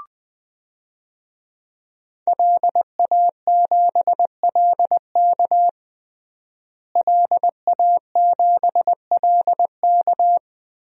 Morse code, heard as a keyed tone, about 20 wpm, 700 Hz. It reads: LA7LK LA7LK
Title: DXCC-100 Call Signs - 5 Characters - Repeated 2x - 20wpm